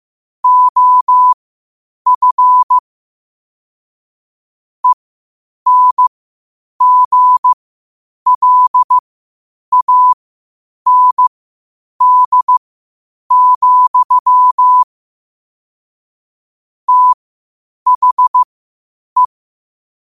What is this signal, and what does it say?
Morse code, heard as a keyed tone, about 15 wpm, spaced as for 8 wpm, 1000 Hz. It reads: OF ENGLAND, THE
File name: qotdfiles / 8wpm